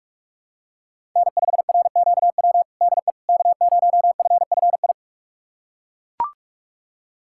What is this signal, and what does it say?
Morse code, heard as a keyed tone, about 41 wpm, 700 Hz. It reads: N5RQW DE K0FFI
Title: USA Callsign DE Callsign 45wpm